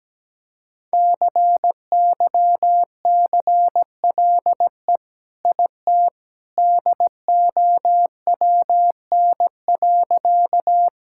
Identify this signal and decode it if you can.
Morse code, heard as a keyed tone, about 17 wpm, 700 Hz. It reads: CYCLE IT DOWN.
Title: Sentences from Top 1000 Words - Review Part 6 - 17wpm